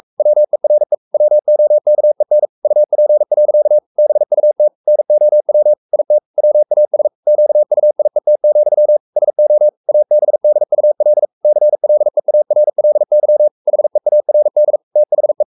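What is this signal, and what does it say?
Morse code, heard as a keyed tone, about 44 wpm, 600 Hz. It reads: WERE WOKEN UP. BUT NOW IT WAS QUIET, SO ABDUL CLEARLY HEARD THE
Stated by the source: Morse code at 43 wpm sending Ken Follett's novel Never in English, from CW med Gnister.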